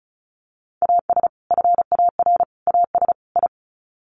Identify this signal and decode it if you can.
Morse code, heard as a keyed tone, about 34 wpm, 700 Hz. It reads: AS FAR AS I
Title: Sets of 4 Words 35wpm